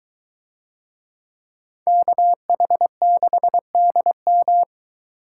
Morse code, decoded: KH6DM

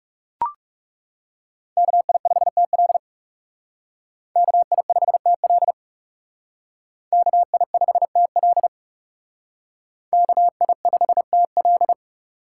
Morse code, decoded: KI5TL KI5TL KI5TL KI5TL

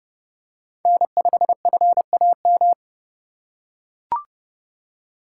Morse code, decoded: N5FAM

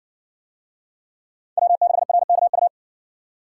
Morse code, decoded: W6RKU

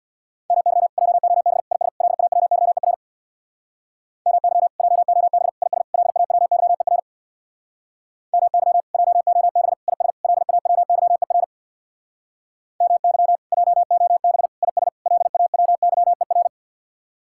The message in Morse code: MY JOB IS LAWYER MY JOB IS LAWYER MY JOB IS LAWYER MY JOB IS LAWYER

